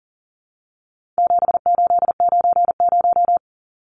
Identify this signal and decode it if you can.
Morse code, heard as a keyed tone, about 40 wpm, 700 Hz. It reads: 7890
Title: Numbers - 4-Digit 40wpm